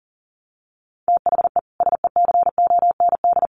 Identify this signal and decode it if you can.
Morse code, heard as a keyed tone, about 40 wpm, 700 Hz. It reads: THE SECOND
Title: Sets of 2 Words 40wpm